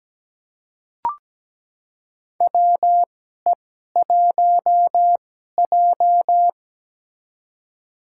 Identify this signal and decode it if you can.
Morse code, heard as a keyed tone, about 17 wpm, spaced as for 12 wpm, 700 Hz. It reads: WE1J
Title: USA Call Signs - 2x Character Spacing - 17wpm